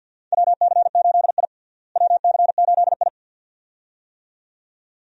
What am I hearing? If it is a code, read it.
Morse code, heard as a keyed tone, about 47 wpm, 700 Hz. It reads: WC8I WC8I
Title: USA Call Signs - 4 Characters - Repeated 2x - 50wpm